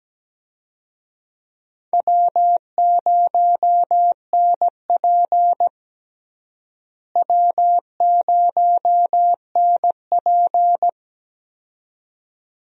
Morse code, heard as a keyed tone, about 17 wpm, 700 Hz. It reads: W0NP W0NP
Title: USA Call Signs - 4 Characters - Repeated 2x - 17wpm